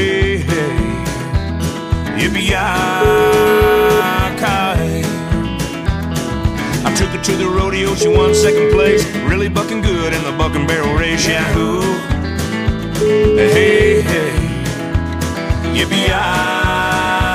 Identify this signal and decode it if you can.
Ringback tone (Europe, 425 Hz), with speech.